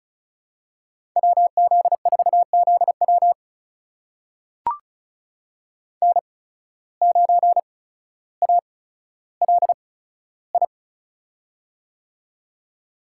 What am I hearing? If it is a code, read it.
Morse code, heard as a keyed tone, about 36 wpm, spaced as for 15 wpm, 700 Hz. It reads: WZ4ZW N9ALI
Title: USA Call Signs - 5 Characters - 8x Character Spacing - 35wpm